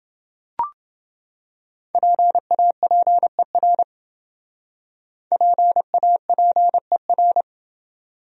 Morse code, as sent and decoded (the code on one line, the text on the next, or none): .--. .- .--. .-. / .--. .- .--. .-.
PAPER PAPER